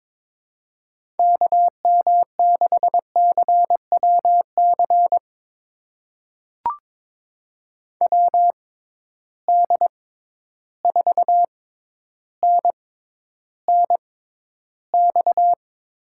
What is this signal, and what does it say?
Morse code, heard as a keyed tone, about 23 wpm, spaced as for 11 wpm, 700 Hz. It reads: KM6CWC WD4NNX
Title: USA Call Signs - 6 Characters - 6x Character Spacing - 22wpm